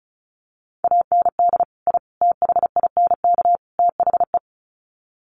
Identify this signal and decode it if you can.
Morse code, heard as a keyed tone, about 34 wpm, 700 Hz. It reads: AND I THINK THE
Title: Sets of 4 Words 35wpm